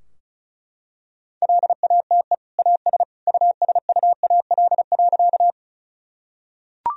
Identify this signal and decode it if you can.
Morse code, heard as a keyed tone, about 36 wpm, 700 Hz. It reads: LATE AS USUAL.
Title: Sentences from Top 1000 Words - Review Part 2 - 35wpm